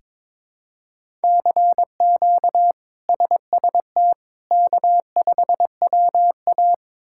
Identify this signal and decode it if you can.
Morse code, heard as a keyed tone, about 22 wpm, 700 Hz. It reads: CQ SST K5WA
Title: K1USN SST QSO Practice - 22wpm